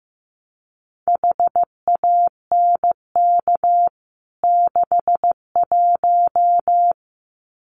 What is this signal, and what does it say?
Morse code, heard as a keyed tone, about 15 wpm, 700 Hz. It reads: HANK 61
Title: CWT Exchange 15wpm